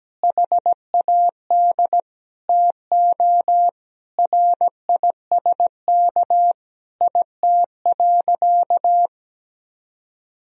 Morse code, heard as a keyed tone, about 17 wpm, 700 Hz. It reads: HAD TO RISK IT.
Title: Sentences from Top 1000 Words - Review Part 4 - 17wpm